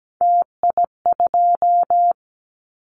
Morse code, heard as a keyed tone, about 17 wpm, 700 Hz. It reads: TI2
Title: Callsign Prefixes for Top 100 DXCC - 17wpm